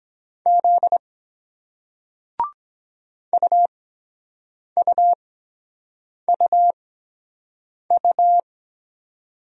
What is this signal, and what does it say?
Morse code, heard as a keyed tone, about 23 wpm, spaced as for 7 wpm, 700 Hz. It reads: Z UUUU